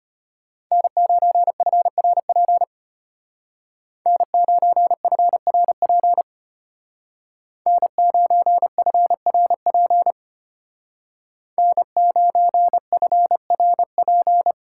Morse code, heard as a keyed tone, about 31 wpm, 700 Hz. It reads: N9FRP N9FRP N9FRP N9FRP
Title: Speed Racing - USA 1x3 Call Signs 25wpm